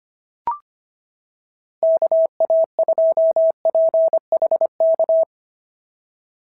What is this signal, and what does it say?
Morse code, heard as a keyed tone, about 25 wpm, 690 Hz. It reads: KA2PHK
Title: SOTA QSO Practice 25wpm